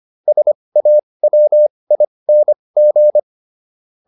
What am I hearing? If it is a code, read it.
Morse code, heard as a keyed tone, about 25 wpm, spaced as for 20 wpm, 600 Hz. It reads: SAWING